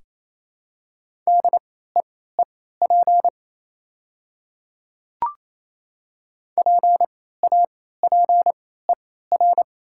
Morse code, heard as a keyed tone, about 28 wpm, spaced as for 16 wpm, 700 Hz. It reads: DEEP PAPER
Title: 3x Character Spacing - Top 500 Words - 28wpm